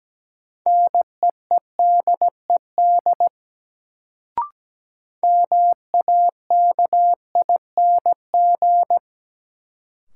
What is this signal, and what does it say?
Morse code, heard as a keyed tone, about 17 wpm, 700 Hz. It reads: NEEDED MAKING
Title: Sentences from Top 1000 Words - Review Part 6 - New Word Drill - 17wpm